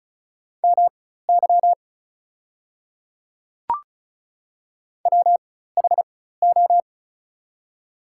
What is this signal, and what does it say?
Morse code, heard as a keyed tone, about 36 wpm, spaced as for 16 wpm, 700 Hz. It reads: MY WHO
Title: 4x Character Spacing - Top 100 Words - 35wpm